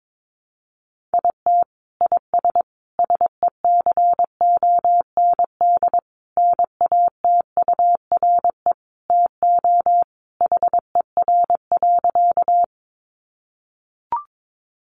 Morse code, heard as a keyed tone, about 22 wpm, 700 Hz. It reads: IT IS SECOND NATURE TO HER.
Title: Sentences from Top 600 Words 22wpm